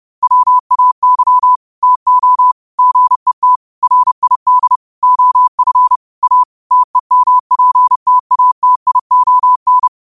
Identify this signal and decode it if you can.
Morse code, heard as a keyed tone, about 30 wpm, 1000 Hz. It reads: WAY TO GET RID OF A TEMPTATION
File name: qotdfiles / 30wpm